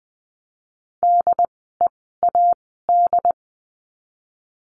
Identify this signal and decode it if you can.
Morse code, heard as a keyed tone, about 20 wpm, spaced as for 15 wpm, 700 Hz. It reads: DEAD
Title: 068 Lesson 12f - Letter d - Words 20wpm